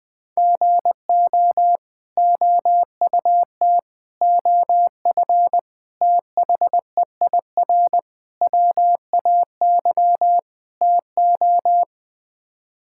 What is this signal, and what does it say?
Morse code, heard as a keyed tone, about 20 wpm, 700 Hz. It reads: GO OUT OF THEIR WAY TO
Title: Sets of 6 Words - 20wpm